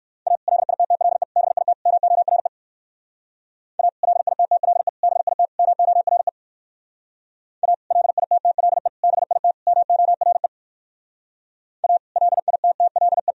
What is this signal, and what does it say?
Morse code, heard as a keyed tone, about 58 wpm, 700 Hz. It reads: A LITTLE BIT MORE A LITTLE BIT MORE A LITTLE BIT MORE A LITTLE